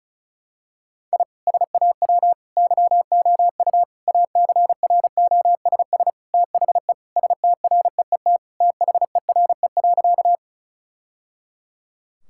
Morse code, heard as a keyed tone, about 36 wpm, 700 Hz. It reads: I SAW YOU ACROSS THE STREET THERE.
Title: Sentences from Top 1000 Words - Review Part 5 - 35wpm